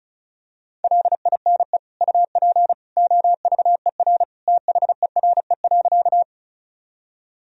Morse code, heard as a keyed tone, about 34 wpm, 700 Hz. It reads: LINE UP OVER THERE.
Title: Sentences from Top 1000 Words - Review Part 1 - 35wpm